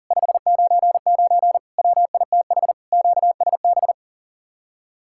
Morse code, heard as a keyed tone, about 52 wpm, 700 Hz. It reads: 599 WITH QSB